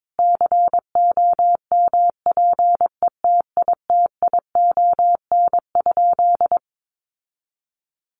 Morse code, heard as a keyed tone, about 22 wpm, 700 Hz. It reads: COMPETITION?
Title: Sentences from Top 1000 Words 22wpm